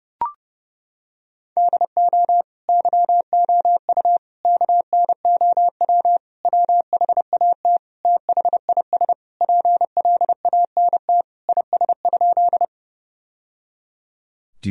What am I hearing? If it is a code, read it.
Morse code, heard as a keyed tone, about 30 wpm, 700 Hz. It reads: DO YOU KNOW WHAT THIS PLANT IS?
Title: Sentences from Top 1000 Words - Review Part 1 - 30wpm